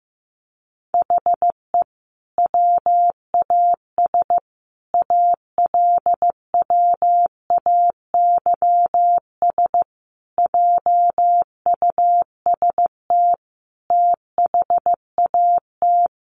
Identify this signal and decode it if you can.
Morse code, heard as a keyed tone, about 15 wpm, 700 Hz. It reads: HE WAS ALWAYS JUST THAT
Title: Sentences from Top 600 Words 15wpm